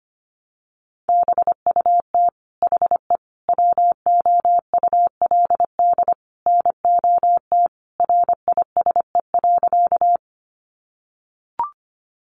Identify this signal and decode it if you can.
Morse code, heard as a keyed tone, about 25 wpm, 700 Hz. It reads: BUT HE WOULD NOT RISE.
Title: Sentences from Top 900 Words 25wpm